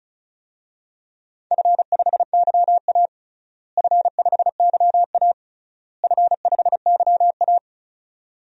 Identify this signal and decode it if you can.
Morse code, heard as a keyed tone, about 34 wpm, 700 Hz. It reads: F5YA F5YA F5YA
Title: DXCC-100 Call Signs - 4 Characters - Repeated 3x - 35wpm